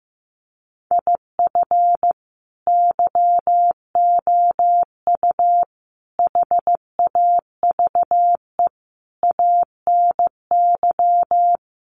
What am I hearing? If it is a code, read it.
Morse code, heard as a keyed tone, about 15 wpm, 700 Hz. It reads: IF YOU HAVE ANY
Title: Sentences from Top 1100 Words 15wpm